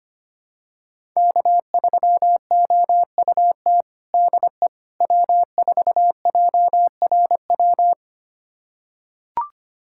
Morse code, heard as a keyed tone, about 25 wpm, 700 Hz. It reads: K3OUT DE W4JRW
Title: USA Callsign DE Callsign - 5 Characters - 25wpm